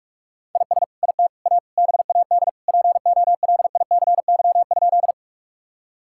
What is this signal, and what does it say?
Morse code, heard as a keyed tone, about 41 wpm, 700 Hz. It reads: IS IT A BAD POLICY?
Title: Sentences from Top 1000 Words - Review Part 5 - 45wpm